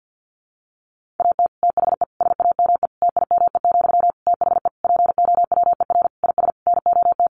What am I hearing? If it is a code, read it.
Morse code, heard as a keyed tone, about 50 wpm, 700 Hz. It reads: AT THE SAME TIME, THE POWER IS NOT
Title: Sentences from Top 1500 Words 50wpm